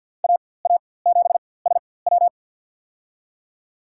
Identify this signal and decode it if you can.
Morse code, heard as a keyed tone, about 51 wpm, spaced as for 23 wpm, 700 Hz. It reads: AA7SW
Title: USA Call Signs - 5 Characters - 4x Character Spacing - 50wpm